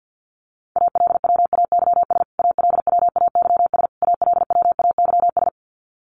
Morse code, heard as a keyed tone, about 50 wpm, 700 Hz. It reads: ALWAYS ALWAYS ALWAYS
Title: Top 500 Words Repeated 50wpm